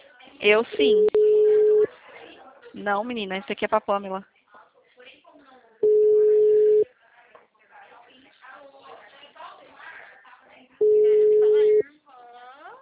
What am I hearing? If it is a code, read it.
Ringback tone (Europe, 425 Hz), with speech.
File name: out- -2002- - - .71565